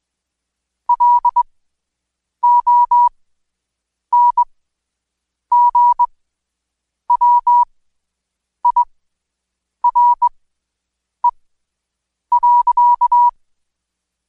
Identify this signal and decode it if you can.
Morse code, heard as a keyed tone, about 21 wpm, spaced as for 7 wpm, 970 Hz. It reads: LONGWIRE.
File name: Code Quick / AudioDownld2 / Seg113-Day18-DayDrill 18